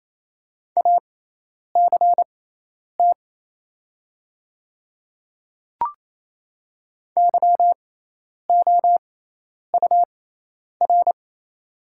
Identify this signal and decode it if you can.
Morse code, heard as a keyed tone, about 27 wpm, spaced as for 10 wpm, 700 Hz. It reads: ACT YOUR
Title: 6x Character Spacing - Top 200 Words - 28wpm